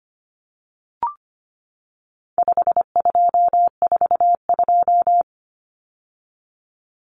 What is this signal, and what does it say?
Morse code, heard as a keyed tone, about 25 wpm, 700 Hz. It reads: 5242
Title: Numbers - 4-Digit 25wpm